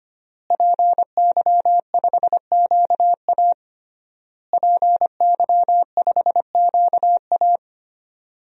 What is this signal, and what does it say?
Morse code, heard as a keyed tone, about 25 wpm, 700 Hz. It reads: PY5QA PY5QA
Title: DXCC-100 Call Signs - 5 Characters - Repeated 3x - 25wpm